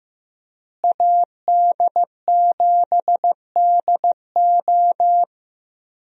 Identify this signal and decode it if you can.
Morse code, heard as a keyed tone, about 15 wpm, 700 Hz. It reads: AD7DO
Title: USA Call Signs - 5 Characters - Repeated 3x - 15wpm